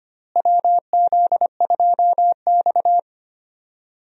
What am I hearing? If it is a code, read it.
Morse code, heard as a keyed tone, about 25 wpm, 700 Hz. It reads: WZ2X
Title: USA Call Signs - 4 Characters - Repeated 2x - 25wpm